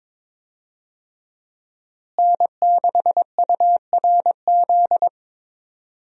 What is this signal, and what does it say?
Morse code, heard as a keyed tone, about 22 wpm, 700 Hz. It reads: N6URZ
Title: USA Call Signs - 5 Characters - Repeated 2x - 22wpm